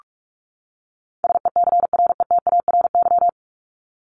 Morse code, heard as a keyed tone, about 44 wpm, 700 Hz. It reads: SECRETARY
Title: Top 1251-1500 Words 45wpm